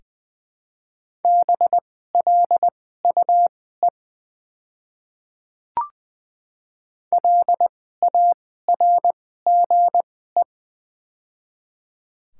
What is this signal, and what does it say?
Morse code, heard as a keyed tone, about 21 wpm, spaced as for 15 wpm, 700 Hz. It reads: BLUE LARGE
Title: 2x Character Spacing - Top 500 Words - 20wpm